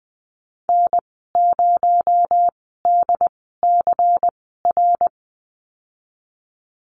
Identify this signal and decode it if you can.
Morse code, heard as a keyed tone, about 20 wpm, spaced as for 14 wpm, 700 Hz. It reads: N0DCR
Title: 238 Lesson 38g - Letter q - Call Signs 20wpm